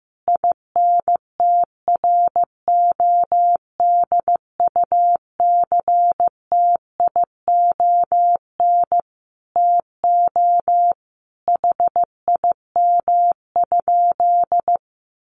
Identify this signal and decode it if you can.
Morse code, heard as a keyed tone, about 15 wpm, 700 Hz. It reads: INTRODUCTION TO HIM?
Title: Sentences from Top 1000 Words 15wpm